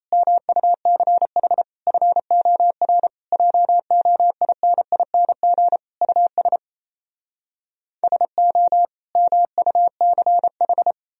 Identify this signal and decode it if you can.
Morse code, heard as a keyed tone, about 38 wpm, 700 Hz. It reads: MUCH FOR JOINING US SO MUCH